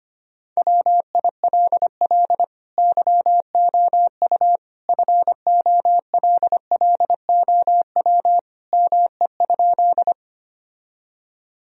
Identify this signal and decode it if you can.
Morse code, heard as a keyed tone, about 25 wpm, 700 Hz. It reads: WILL YOU FOLLOW ME?
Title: Sentences from Top 1000 Words - Review Part 1 - 25wpm